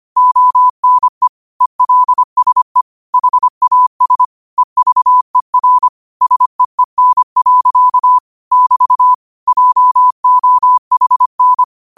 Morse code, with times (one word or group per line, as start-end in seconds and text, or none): 0.2-1.3 ONE
1.6-2.8 ELSE
3.1-4.2 HAS
4.6-5.9 EVER
6.2-8.2 SEEN.
8.5-9.1 <BT>
9.5-11.6 JOHN